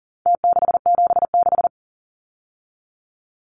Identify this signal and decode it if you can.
Morse code, heard as a keyed tone, about 40 wpm, 700 Hz. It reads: T676
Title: Numbers - Contest Style - 4-Digit 40wpm